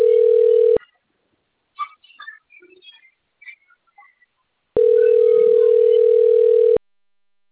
Ringback tone (North America), 440 + 480 Hz.
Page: Rio Claro